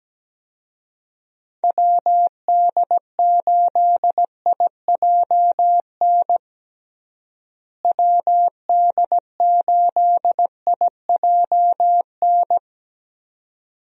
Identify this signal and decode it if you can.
Morse code, heard as a keyed tone, about 17 wpm, 700 Hz. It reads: WD8IJN WD8IJN
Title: USA Call Signs - 6 Characters - Repeated 2x - 17wpm